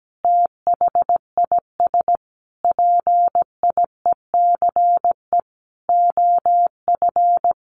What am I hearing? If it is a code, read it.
Morse code, heard as a keyed tone, about 17 wpm, 700 Hz. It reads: THIS PIECE OF
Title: Sentences from Top 400 Words 17wpm